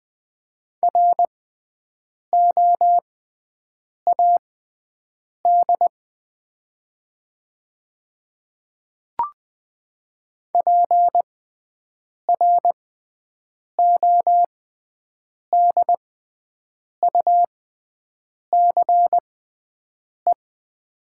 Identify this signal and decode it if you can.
Morse code, heard as a keyed tone, about 20 wpm, spaced as for 7 wpm, 700 Hz. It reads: ROAD PRODUCE
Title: 6x Character Spacing - Top 500 Words - 20wpm